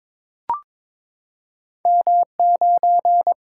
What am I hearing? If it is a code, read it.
Morse code, heard as a keyed tone, about 22 wpm, 700 Hz. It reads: M 9
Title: Single-Letter Single-Number 22wpm